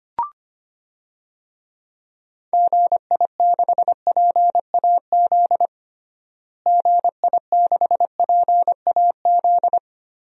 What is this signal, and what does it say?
Morse code, heard as a keyed tone, about 25 wpm, 700 Hz. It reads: GI6PAZ GI6PAZ
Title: DXCC-100 Call Signs - 6 Characters - Repeated 3x - 25wpm